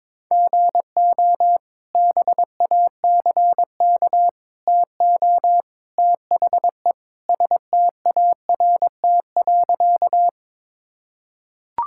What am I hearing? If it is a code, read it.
Morse code, heard as a keyed tone, about 22 wpm, 700 Hz. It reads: GO BACK TO THE START.
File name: practice / 22wpm / Sentences with 3-Word Phrases - 22wpm